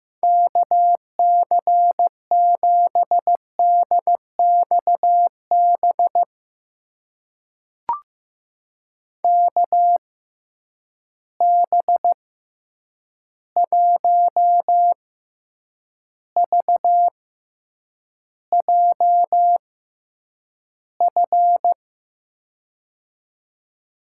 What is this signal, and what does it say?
Morse code, heard as a keyed tone, about 15 wpm, spaced as for 8 wpm, 700 Hz. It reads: KC7DXB KB1VJF
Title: USA Call Signs - 6 Characters - 6x Character Spacing - 15wpm